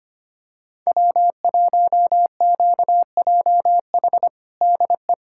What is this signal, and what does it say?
Morse code, heard as a keyed tone, about 25 wpm, 700 Hz. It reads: W1QJH DE
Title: USA Callsign DE Callsign 25wpm